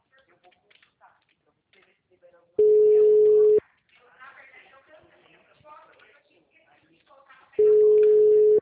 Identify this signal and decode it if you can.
Ringback tone (Europe, 425 Hz), with speech.
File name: out- -2002- - - .46203